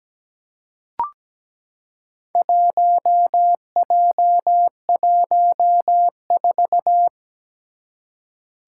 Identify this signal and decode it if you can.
Morse code, heard as a keyed tone, about 17 wpm, 700 Hz. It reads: 1J14
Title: Four Challenging Letter-Number Combinations 17wpm